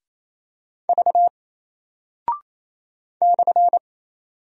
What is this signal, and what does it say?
Morse code, heard as a keyed tone, about 27 wpm, 700 Hz. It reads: V /
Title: Single Character (Letter:Number:Punctuation:Prosign) 28wpm